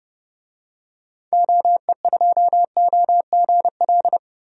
Morse code, heard as a keyed tone, about 30 wpm, 700 Hz. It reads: OE2OGL
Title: DXCC-100 Call Signs - 6 Characters - Repeated 2x - 30wpm